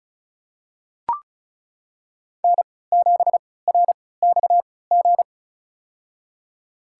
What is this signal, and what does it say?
Morse code, heard as a keyed tone, about 34 wpm, spaced as for 20 wpm, 700 Hz. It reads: N7RXG
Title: USA Call Signs - 3x Character Spacing - 35wpm